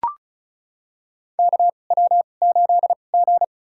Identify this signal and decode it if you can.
Morse code, heard as a keyed tone, about 36 wpm, spaced as for 25 wpm, 700 Hz. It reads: KW8G
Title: USA Call Signs - 4 Characters - 2x Character Spacing - 35wpm